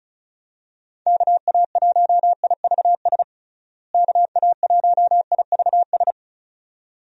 Morse code, heard as a keyed tone, about 36 wpm, 700 Hz. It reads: KA1IVS KA1IVS
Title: USA Call Signs - 6 Characters - Repeated 2x - 35wpm